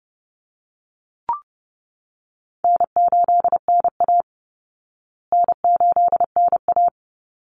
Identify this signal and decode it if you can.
Morse code, heard as a keyed tone, about 28 wpm, 700 Hz. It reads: N8NA N8NA
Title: Easy Call Signs Repeated 2x - 30wpm